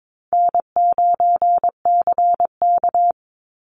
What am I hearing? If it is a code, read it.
Morse code, heard as a keyed tone, about 22 wpm, 700 Hz. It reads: N9CK
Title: Easy Call Signs Repeated 3x - 22wpm